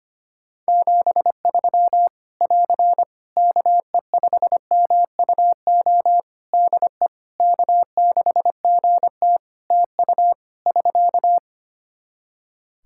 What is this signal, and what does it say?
Morse code, heard as a keyed tone, about 25 wpm, 700 Hz. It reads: 73 <AR> KE5MUO DE K6GT TU <SK>
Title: Practice QSO - Exchange 4 - Ending - 25wpm